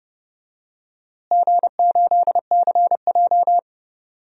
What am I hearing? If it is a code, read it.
Morse code, heard as a keyed tone, about 30 wpm, 700 Hz. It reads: G8CJ
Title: DXCC-100 Call Signs - 4 Characters - Repeated 3x - 30wpm